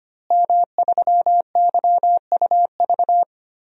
Morse code, heard as a keyed tone, about 25 wpm, 700 Hz. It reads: M3YUV
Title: DXCC-100 Call Signs - 5 Characters - Repeated 2x - 25wpm